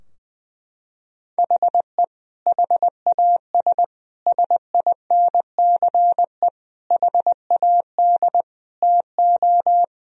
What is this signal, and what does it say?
Morse code, heard as a keyed tone, about 21 wpm, 700 Hz. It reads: HE HAS SINCE HAD TO
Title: Sentences from Top 1000 Words - Review Part 2 - 20wpm